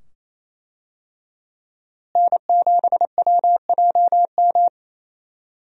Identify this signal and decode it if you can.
Morse code, heard as a keyed tone, about 27 wpm, 700 Hz. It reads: N7WJM
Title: USA Call Signs - 5 Characters - Repeated 2x - 28wpm